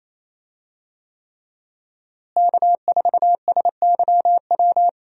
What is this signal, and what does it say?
Morse code, heard as a keyed tone, about 27 wpm, 700 Hz. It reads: K4SYW
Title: USA Call Signs - 5 Characters - Repeated 3x - 28wpm